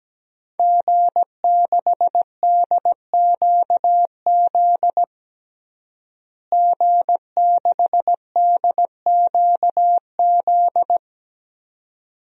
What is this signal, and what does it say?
Morse code, heard as a keyed tone, about 17 wpm, 700 Hz. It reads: G6DQZ G6DQZ
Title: DXCC-100 Call Signs - 5 Characters - Repeated 2x - 17wpm